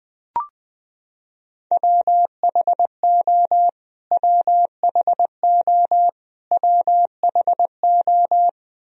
Morse code, heard as a keyed tone, about 20 wpm, 700 Hz. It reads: WHO WHO WHO
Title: Top 100 Words Repeated 20wpm